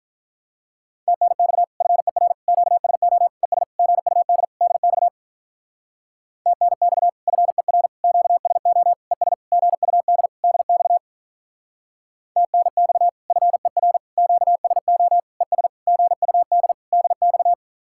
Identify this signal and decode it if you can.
Morse code, heard as a keyed tone, about 45 wpm, 700 Hz. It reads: TNX FER QSO ES GUD DX TNX FER QSO ES GUD DX TNX FER QSO ES GUD DX